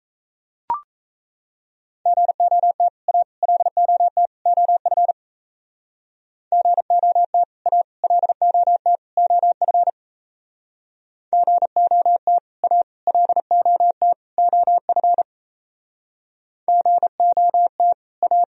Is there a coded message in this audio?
Yes: Morse code from 2.1 to 18.4 s, heard as a keyed tone, about 35 wpm, 700 Hz. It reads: GOT A LOT OF GOT A LOT OF GOT A LOT OF GOT A